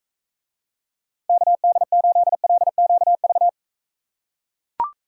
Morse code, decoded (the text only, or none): KD8LQV